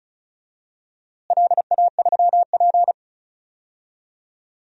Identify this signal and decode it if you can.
Morse code, heard as a keyed tone, about 34 wpm, 700 Hz. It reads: LA3P
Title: DXCC-100 Call Signs - 4 Characters - Repeated 3x - 35wpm